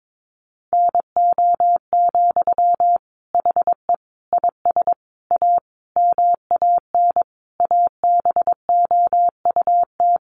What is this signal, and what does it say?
Morse code, heard as a keyed tone, about 22 wpm, 700 Hz. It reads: NO, HE IS A MAN ABOUT
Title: Sentences from Top 400 Words 22wpm